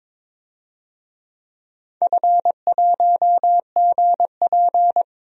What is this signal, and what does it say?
Morse code, heard as a keyed tone, about 21 wpm, 700 Hz. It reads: F1GP
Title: DXCC-100 Call Signs - 4 Characters - Repeated 3x - 22wpm